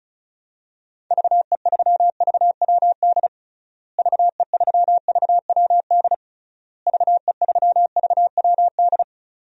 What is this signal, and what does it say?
Morse code, heard as a keyed tone, about 36 wpm, 700 Hz. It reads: VE3VWD VE3VWD VE3VWD
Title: DXCC-100 Call Signs - 6 Characters - Repeated 3x - 35wpm